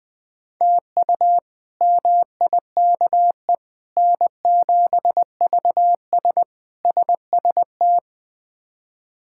Morse code, heard as a keyed tone, about 20 wpm, 700 Hz. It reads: TU MIKE N7VS SST
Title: K1USN SST QSO Practice - 20wpm